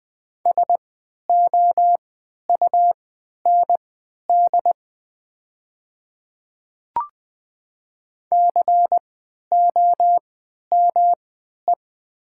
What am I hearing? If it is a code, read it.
Morse code, heard as a keyed tone, about 20 wpm, spaced as for 11 wpm, 690 Hz. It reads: SOUND COME
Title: 3x Character Spacing - Top 100 Words - 20wpm